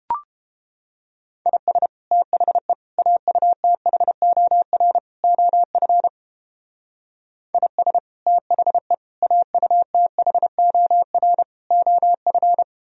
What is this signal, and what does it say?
Morse code, heard as a keyed tone, about 30 wpm, 700 Hz. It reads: IS THE AUTHOR OF IS THE AUTHOR OF